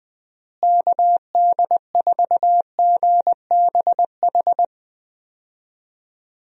Morse code, heard as a keyed tone, about 20 wpm, 700 Hz. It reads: KD4GBH
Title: USA Call Signs - 6 Characters - Repeated 2x - 20wpm